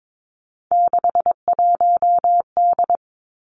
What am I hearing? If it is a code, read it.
Morse code, heard as a keyed tone, about 22 wpm, 700 Hz. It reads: 61D
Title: Three Challenging Letter-Number Combinations 22wpm